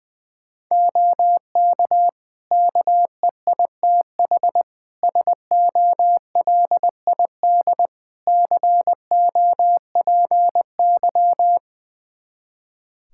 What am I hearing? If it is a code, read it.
Morse code, heard as a keyed tone, about 21 wpm, 700 Hz. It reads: OK KEITH SOLID COPY
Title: Practice QSO - Exchange 3 - More Info - 20wpm